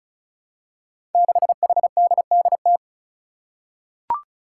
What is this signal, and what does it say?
Morse code, heard as a keyed tone, about 36 wpm, 700 Hz. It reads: 6HDDT
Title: Five Challenging Letter-Number Combinations 35wpm